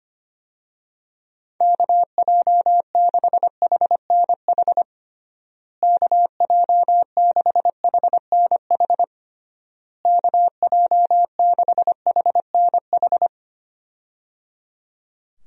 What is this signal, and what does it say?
Morse code, heard as a keyed tone, about 25 wpm, 700 Hz. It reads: KJ6HNH KJ6HNH KJ6HNH
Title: USA Call Signs - 6 Characters - Repeated 3x - 25wpm